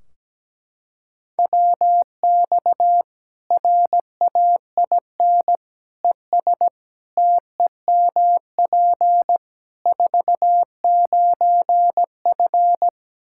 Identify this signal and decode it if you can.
Morse code, heard as a keyed tone, about 17 wpm, 700 Hz. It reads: WX RAIN ES TEMP 49F
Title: Practice QSO - Exchange 2 - Info - 17wpm